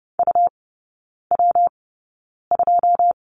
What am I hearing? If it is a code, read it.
Morse code, heard as a keyed tone, about 30 wpm, spaced as for 9 wpm, 700 Hz. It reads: UW2
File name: Lesson 19e - Letter w - Three Characters 30wpm